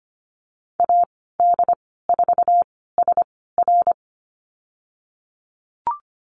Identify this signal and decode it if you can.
Morse code, heard as a keyed tone, about 25 wpm, spaced as for 16 wpm, 700 Hz. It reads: AD4SR
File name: Lesson 15g - Number 5 - Call Signs 25wpm